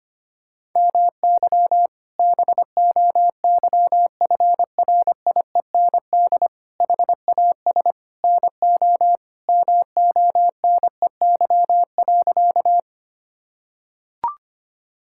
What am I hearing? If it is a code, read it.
Morse code, heard as a keyed tone, about 25 wpm, 700 Hz. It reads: MY BOYFRIEND HAS NO MONEY.
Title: Sentences from Top 1000 Words - Review Part 5 - 25wpm